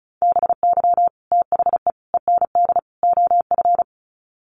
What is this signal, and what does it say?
Morse code, heard as a keyed tone, about 34 wpm, 700 Hz. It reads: BY THE END OF
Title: Sets of 4 Words 35wpm